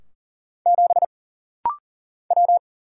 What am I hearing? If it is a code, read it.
Morse code, heard as a keyed tone, about 40 wpm, 700 Hz. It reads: 7 W
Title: Single Letter-Number - Rapid-Fire 40wpm